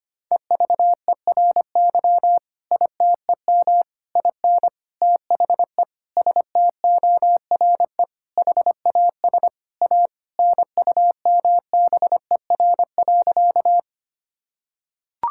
Morse code, heard as a keyed tone, about 25 wpm, 700 Hz. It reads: EVERY ITEM IN THE STORE HAS A NUMBER.
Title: Sentences from Top 1000 Words - Review Part 6 - 25wpm